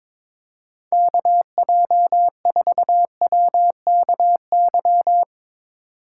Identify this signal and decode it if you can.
Morse code, heard as a keyed tone, about 22 wpm, 700 Hz. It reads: KJ4WKY
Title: Speed Racing - USA 2x3 Call Signs 22wpm